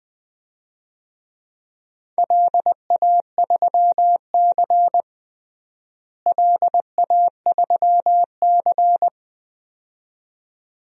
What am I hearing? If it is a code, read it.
Morse code, heard as a keyed tone, about 20 wpm, 700 Hz. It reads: LA3C LA3C
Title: DXCC-100 Call Signs - 4 Characters - Repeated 2x - 20wpm